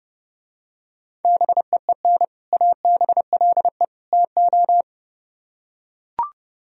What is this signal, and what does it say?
Morse code, heard as a keyed tone, about 32 wpm, 700 Hz. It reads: BEEN ABLE TO